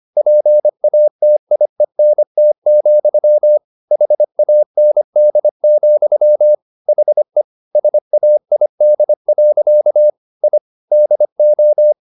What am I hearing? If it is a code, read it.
Morse code, heard as a keyed tone, about 25 wpm, 600 Hz. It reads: PATIENT, HAND, HE SAID. I DO
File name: the old man and the sea part 1 25wpm